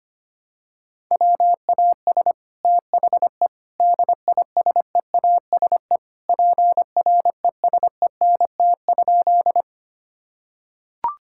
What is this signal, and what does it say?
Morse code, heard as a keyed tone, about 25 wpm, 700 Hz. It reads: WAS THE DISEASE PRESENT?
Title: Sentences from Top 1000 Words - Review Part 6 - 25wpm